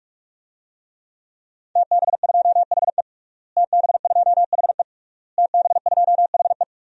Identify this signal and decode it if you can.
Morse code, heard as a keyed tone, about 46 wpm, 700 Hz. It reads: TB2HE TB2HE TB2HE
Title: DXCC-100 Call Signs - 5 Characters - Repeated 3x - 45wpm